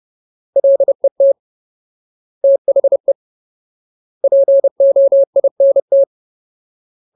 Morse code, heard as a keyed tone, about 30 wpm, 550 Hz. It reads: LET THE POINT
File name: Ultimate Word List / 30WPM Ultimate